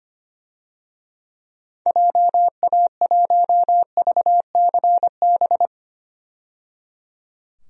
Morse code, heard as a keyed tone, about 25 wpm, 690 Hz. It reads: JA1VCB
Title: DXCC-100 Call Signs - 6 Characters - 25wpm